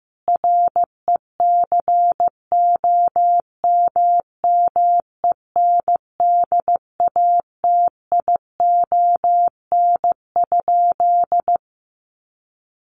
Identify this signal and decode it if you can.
Morse code, heard as a keyed tone, about 15 wpm, 700 Hz. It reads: RECOMMENDATION?
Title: Sentences from Top 1400 Words 15wpm